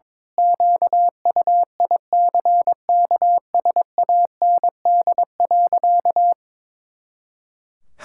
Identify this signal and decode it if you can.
Morse code, heard as a keyed tone, about 22 wpm, 700 Hz. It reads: QUICKSAND.
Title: Pangram Challenge v2 - 22wpm